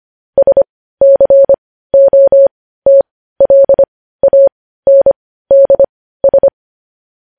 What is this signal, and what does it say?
Morse code, heard as a keyed tone, about 25 wpm, spaced as for 15 wpm, 550 Hz. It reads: SCOTLANDS
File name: podcast / News headlines-1x-25x15